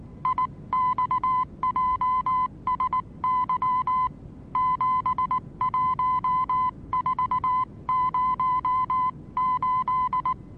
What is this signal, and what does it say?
Morse code, heard as a keyed tone, about 19 wpm, 1010 Hz. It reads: IXJSY 71408
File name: repos / Morsekurs / DJ4UF Morsen26